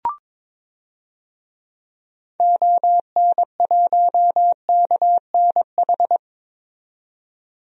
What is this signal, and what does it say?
Morse code, heard as a keyed tone, about 22 wpm, 700 Hz. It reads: ON1KNH
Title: DXCC-100 Call Signs - 6 Characters - 22wpm